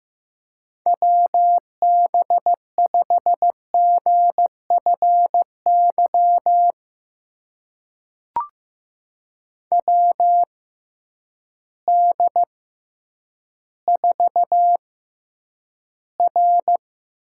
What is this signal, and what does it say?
Morse code, heard as a keyed tone, about 15 wpm, spaced as for 9 wpm, 700 Hz. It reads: WB5GFY WD4R
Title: USA Call Signs - 6 Characters - 6x Character Spacing - 15wpm